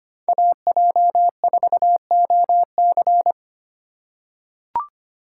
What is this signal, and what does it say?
Morse code, heard as a keyed tone, about 25 wpm, 700 Hz. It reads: AJ4OC
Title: USA Call Signs - 5 Characters - 3x Character Spacing - 25wpm